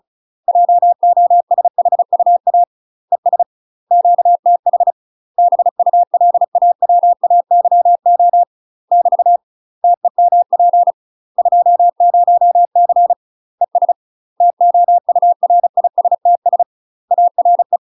Morse code, heard as a keyed tone, about 36 wpm, 700 Hz. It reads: JOSHUA ES QTH BULAWAYO <BT> TEMP 20C ES TOURISTS ARE
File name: UK MASTER QSOs TRAINING / 35WPM / UK MASTER QSOs 0020